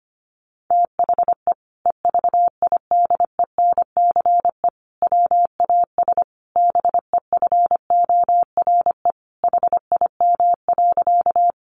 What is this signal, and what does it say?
Morse code, heard as a keyed tone, about 25 wpm, 700 Hz. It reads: THE EVIDENCE WAS BEFORE HIM.
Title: Sentences from Top 1300 Words 25wpm